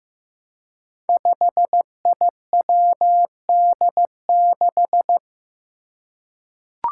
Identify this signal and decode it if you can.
Morse code, heard as a keyed tone, about 15 wpm, 700 Hz. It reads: 5IWD6
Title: Five Challenging Letter-Number Combinations 15wpm